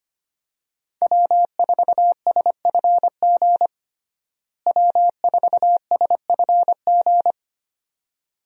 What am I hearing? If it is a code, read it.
Morse code, heard as a keyed tone, about 25 wpm, 700 Hz. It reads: W4SFG W4SFG
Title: USA Call Signs - 5 Characters - Repeated 2x - 25wpm